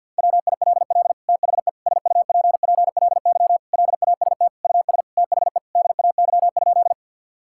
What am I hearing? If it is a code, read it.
Morse code, heard as a keyed tone, about 50 wpm, 700 Hz. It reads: WILL THE SUPPLY LAST US THE DAY?
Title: Sentences from Top 1000 Words - Review Part 5 - 50wpm